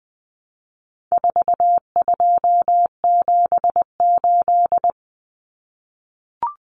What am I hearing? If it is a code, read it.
Morse code, heard as a keyed tone, about 20 wpm, 700 Hz. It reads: 4278
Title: Numbers - 4-Digit 20wpm